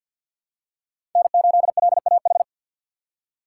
Morse code, heard as a keyed tone, about 51 wpm, 700 Hz. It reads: N8LAH
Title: USA Call Signs - 5 Characters - 4x Character Spacing - 50wpm